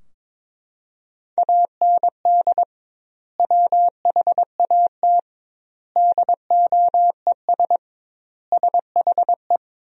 Morse code, heard as a keyed tone, about 22 wpm, 700 Hz. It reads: AND WHAT DOES SHE
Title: 2x Word-Spacing - Sentences from Top 400 Words - 22wpm